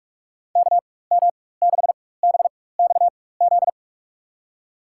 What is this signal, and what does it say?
Morse code, heard as a keyed tone, about 44 wpm, spaced as for 21 wpm, 700 Hz. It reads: KM6BXZ
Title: USA Call Signs - 6 Characters - 4x Character Spacing - 45wpm